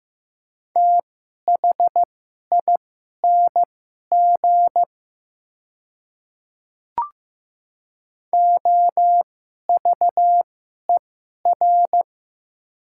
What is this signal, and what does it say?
Morse code, heard as a keyed tone, about 15 wpm, spaced as for 11 wpm, 700 Hz. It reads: THING OVER
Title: 2x Character Spacing - Top 200 Words - 15wpm